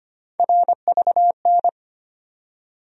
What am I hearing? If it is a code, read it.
Morse code, heard as a keyed tone, about 25 wpm, 700 Hz. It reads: RVN
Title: USA Call Sign Postfixes - 3 Letters 25wpm